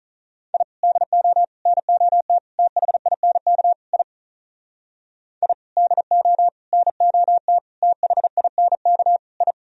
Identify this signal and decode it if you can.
Morse code, heard as a keyed tone, about 43 wpm, 700 Hz. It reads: I DO NOT THINK I I DO NOT THINK I
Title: Speed-Racing - Sets of 5 Words 35wpm